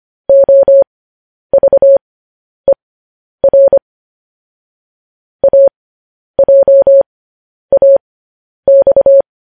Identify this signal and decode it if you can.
Morse code, heard as a keyed tone, about 25 wpm, spaced as for 10 wpm, 550 Hz. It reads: OVER AJAX
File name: podcast / News headlines-1x-25x10